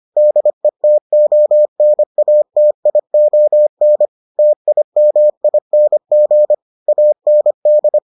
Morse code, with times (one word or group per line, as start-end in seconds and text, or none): 0.2-4.1 DETONATION
4.4-6.5 TIMING
6.9-8.0 AND